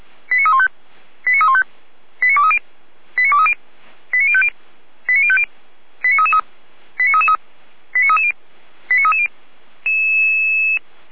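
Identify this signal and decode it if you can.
Five-tone selective call (ZVEI), heard as quick runs of five tones, tones of about 70 ms. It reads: 89315 89315 89230 89230 89050 89050 89333 89330 89330
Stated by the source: ZVEI five-tone selective call 89304, received via SDR.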